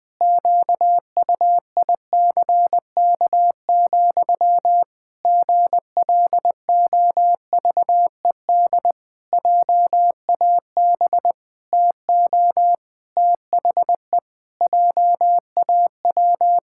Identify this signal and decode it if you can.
Morse code, heard as a keyed tone, about 20 wpm, 700 Hz. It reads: QUICK, GLOVED JAB TO THE JAW
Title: Pangram Challenge v2 - 20wpm